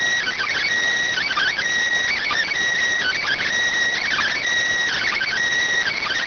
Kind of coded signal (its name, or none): Emergency Alert System (SAME) tones